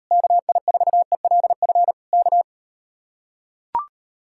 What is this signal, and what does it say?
Morse code, heard as a keyed tone, about 38 wpm, 700 Hz. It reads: KI4ELF K